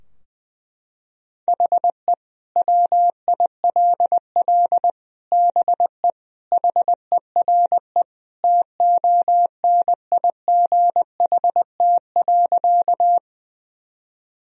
Morse code, heard as a keyed tone, about 20 wpm, 700 Hz. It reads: HE WILL BE HERE TONIGHT.
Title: Sentences from Top 1000 Words - Review Part 3 - 20wpm